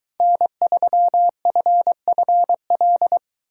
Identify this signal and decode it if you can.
Morse code, heard as a keyed tone, about 22 wpm, 700 Hz. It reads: N3FFL